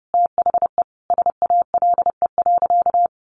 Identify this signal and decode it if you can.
Morse code, heard as a keyed tone, about 30 wpm, 700 Hz. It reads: THE SALE.
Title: Sentences from Top 1400 Words 30wpm